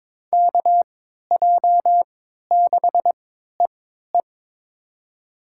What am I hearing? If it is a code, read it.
Morse code, heard as a keyed tone, about 22 wpm, spaced as for 12 wpm, 700 Hz. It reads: KJ6EE
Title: USA Call Signs - 5 Characters - 3x Character Spacing - 22wpm